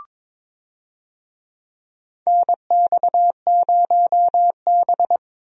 Morse code, heard as a keyed tone, about 22 wpm, 700 Hz. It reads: NX0B
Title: USA Call Signs - 4 Characters - Repeated 3x - 22wpm